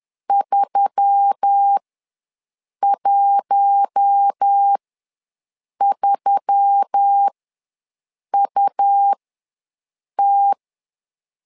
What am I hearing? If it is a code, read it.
Morse code, heard as a keyed tone, about 10 wpm, spaced as for 6 wpm, 800 Hz. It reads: 313UT
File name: LEZIONI / 4 Q1S3 ws 10wpm 2437